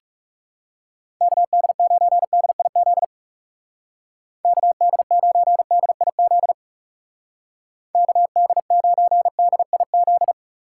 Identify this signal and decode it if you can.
Morse code, heard as a keyed tone, about 35 wpm, 700 Hz. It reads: KD9DIZ KD9DIZ KD9DIZ
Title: Speed Racing - USA 2x3 Call Signs 30wpm